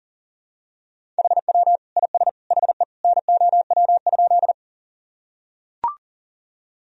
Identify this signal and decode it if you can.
Morse code, heard as a keyed tone, about 40 wpm, 700 Hz. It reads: HW IS HE NOW?
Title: Sentences from Top 100 Words and Abbreviations 40wpm